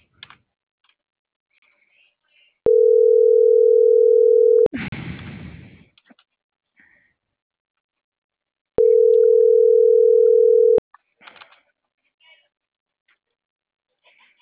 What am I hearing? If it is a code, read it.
Ringback tone (North America), 440 + 480 Hz.